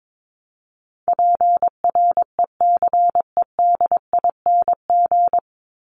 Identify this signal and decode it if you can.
Morse code, heard as a keyed tone, about 22 wpm, 700 Hz. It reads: PRECEDING
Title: ING Suffix - Top 100 Words with Base Words 22wpm